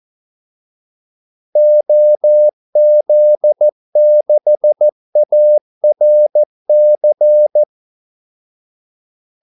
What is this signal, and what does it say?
Morse code, heard as a keyed tone, about 14 wpm, 600 Hz. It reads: OZ6ARC
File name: Callsigns-14wpm